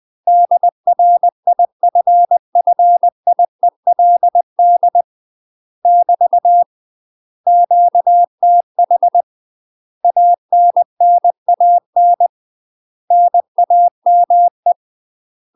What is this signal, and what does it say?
Morse code, heard as a keyed tone, about 20 wpm, 700 Hz. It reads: DRIFFIELD <BT> QTH ANNAN NAME
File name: UK MASTER QSOs TRAINING / 20WPM / UK MASTER QSOs 0025